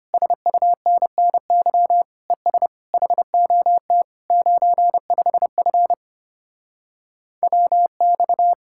Morse code, heard as a keyed tone, about 33 wpm, 700 Hz. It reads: SUNNY ES HOT 95F WX